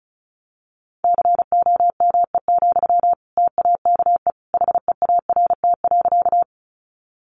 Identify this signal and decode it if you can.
Morse code, heard as a keyed tone, about 34 wpm, 700 Hz. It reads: COME, TAKE HEART.
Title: Sentences from Top 1200 Words 35wpm